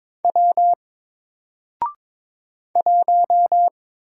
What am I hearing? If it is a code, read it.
Morse code, heard as a keyed tone, about 22 wpm, 700 Hz. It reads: W 1
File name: Single-Letter-Number-22wpm